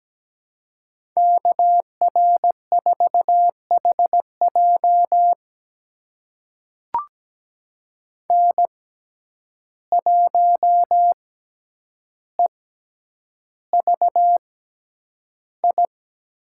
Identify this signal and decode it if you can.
Morse code, heard as a keyed tone, about 17 wpm, spaced as for 9 wpm, 700 Hz. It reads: KR4HJ N1EVI
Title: USA Call Signs - 5 Characters - 6x Character Spacing - 17wpm